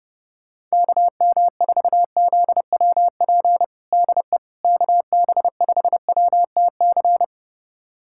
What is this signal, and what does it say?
Morse code, heard as a keyed tone, about 30 wpm, 700 Hz. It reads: KM4ZWP DE KB5WTC
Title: USA Callsign DE Callsign 30wpm